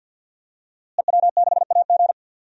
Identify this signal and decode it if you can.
Morse code, heard as a keyed tone, about 48 wpm, 700 Hz. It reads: EW6AG
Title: DXCC-100 Call Signs - 5 Characters - Repeated 3x - 50wpm